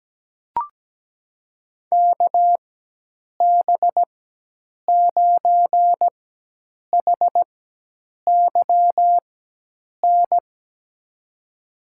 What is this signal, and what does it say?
Morse code, heard as a keyed tone, about 17 wpm, spaced as for 8 wpm, 700 Hz. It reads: KB9HYN
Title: USA Call Signs - 4x Character Spacing - 17wpm